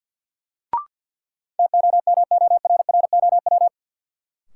Morse code, heard as a keyed tone, about 50 wpm, 700 Hz. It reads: TOMORROW
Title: Sentences from Top 1100 Words - New Word Drill - 50wpm